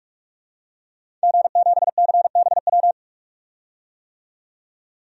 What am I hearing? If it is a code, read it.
Morse code, heard as a keyed tone, about 41 wpm, 700 Hz. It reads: G7CBW
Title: DXCC-100 Call Signs - 5 Characters - 45wpm